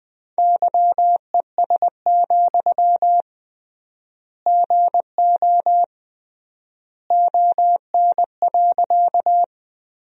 Morse code, heard as a keyed tone, about 21 wpm, 700 Hz. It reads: YES, GO ON.
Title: 3x Word-Spacing - Sentences from Top 500 Words - 20wpm